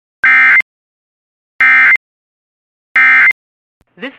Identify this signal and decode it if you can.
Emergency Alert System (SAME) tones, heard as harsh buzzing bursts. It is heard from 0.2 to 3.3 s.